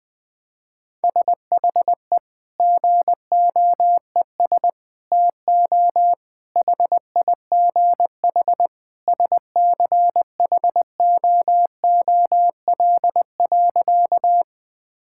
Morse code, heard as a keyed tone, about 20 wpm, 700 Hz. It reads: SHE GOES TO HIGH SCHOOL.
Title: Sentences with 2-Word Phrases - 20wpm